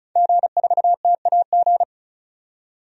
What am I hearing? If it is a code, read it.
Morse code, heard as a keyed tone, about 34 wpm, 700 Hz. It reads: G4TAG
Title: DXCC-100 Call Signs - 5 Characters - 35wpm